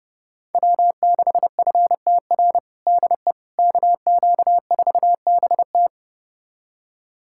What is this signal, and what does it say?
Morse code, heard as a keyed tone, about 30 wpm, 700 Hz. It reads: W6FTR DE KQ4BT
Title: USA Callsign DE Callsign - 5 Characters - 30wpm